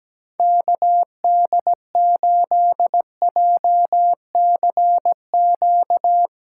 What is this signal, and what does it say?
Morse code, heard as a keyed tone, about 17 wpm, 700 Hz. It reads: KD8JCQ
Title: USA Call Signs - 6 Characters - 2x Character Spacing - 17wpm